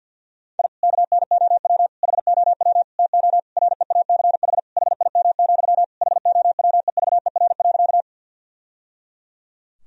Morse code, heard as a keyed tone, about 51 wpm, 700 Hz. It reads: I KNOW HOW TO REACH HIM, HOWEVER.
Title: Sentences from Top 1000 Words - Review Part 2 - 50wpm